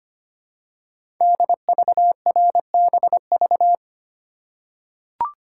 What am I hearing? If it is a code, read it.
Morse code, heard as a keyed tone, about 25 wpm, 700 Hz. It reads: DVRBV
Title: Code Groups - 5 Letters - 25wpm